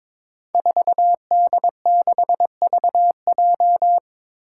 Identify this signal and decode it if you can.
Morse code, heard as a keyed tone, about 22 wpm, 700 Hz. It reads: 4D6VJ
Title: Five Challenging Letter-Number Combinations 22wpm